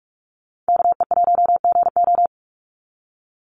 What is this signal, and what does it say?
Morse code, heard as a keyed tone, about 46 wpm, 700 Hz. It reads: KE1GO
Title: USA Call Signs 45wpm